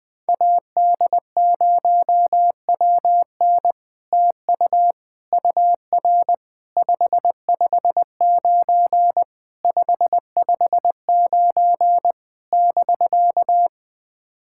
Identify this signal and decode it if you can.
Morse code, heard as a keyed tone, about 20 wpm, 700 Hz. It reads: AD0WN TU UR 559 559 <BK>
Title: POTA QSO Practice - 20wpm